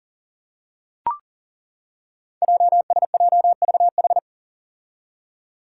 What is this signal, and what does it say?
Morse code, heard as a keyed tone, about 38 wpm, 700 Hz. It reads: JSJVH
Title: Five Challenging Letter-Number Combinations 40wpm